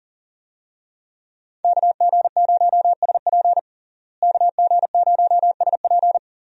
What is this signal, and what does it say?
Morse code, heard as a keyed tone, about 40 wpm, 700 Hz. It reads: KG0SP KG0SP
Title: USA Call Signs - 5 Characters - Repeated 3x - 40wpm